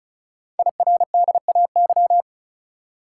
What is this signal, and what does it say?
Morse code, heard as a keyed tone, about 36 wpm, 700 Hz. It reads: IRDAY
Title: Code Groups - 5 Letters - 35wpm